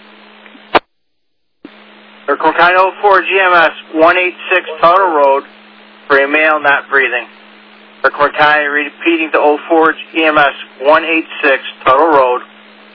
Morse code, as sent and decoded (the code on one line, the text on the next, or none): none